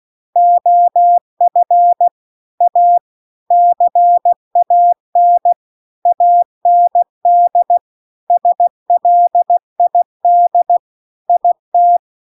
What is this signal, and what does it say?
Morse code, heard as a keyed tone, about 16 wpm, 700 Hz. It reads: OF A CAN AND SLID IT